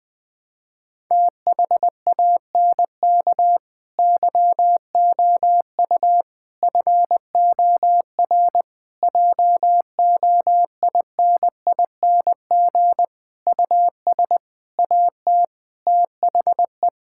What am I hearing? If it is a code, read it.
Morse code, heard as a keyed tone, about 20 wpm, 700 Hz. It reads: THANK YOU FOR JOINING US AT THE
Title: Sentences with 5-Word Phrases - 20wpm